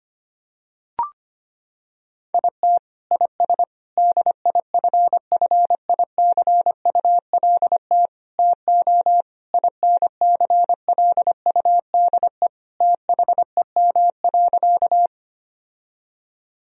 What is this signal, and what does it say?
Morse code, heard as a keyed tone, about 25 wpm, 700 Hz. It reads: IT IS DIFFICULT TO INCLUDE THEM.
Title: Sentences from Top 1000 Words - Review Part 3 - 25wpm